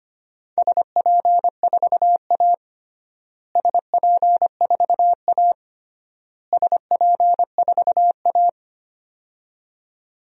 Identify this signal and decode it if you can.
Morse code, heard as a keyed tone, about 25 wpm, 700 Hz. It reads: SP4A SP4A SP4A
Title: DXCC-100 Call Signs - 4 Characters - Repeated 3x - 25wpm